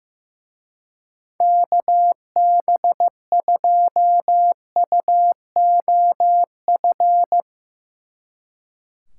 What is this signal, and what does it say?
Morse code, heard as a keyed tone, about 15 wpm, 700 Hz. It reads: KB2UOF
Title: Speed Racing - USA 2x3 Call Signs 15wpm